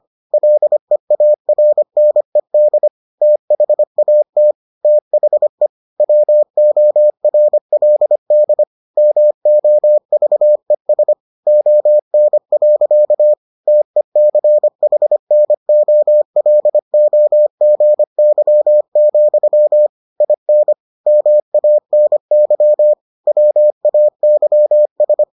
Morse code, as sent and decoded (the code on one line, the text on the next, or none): .-.. .- .-. -. -.. / - .... .- - / - .... / .-- --- .-. .-.. -.. / -- --- ...- . ... / --- -. .-.-.- / - . -.-. .... -. --- .-.. --- --. -.-- --..-- / .. -. / -- .- -. -.-- / .-- .- -.-- ...
LEARNED THAT THE WORLD MOVES ON. TECHNOLOGY, IN MANY WAYS